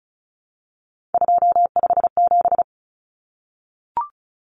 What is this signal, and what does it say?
Morse code, heard as a keyed tone, about 34 wpm, 700 Hz. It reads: 257
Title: Common Cut Numbers - 3-Digit 35wpm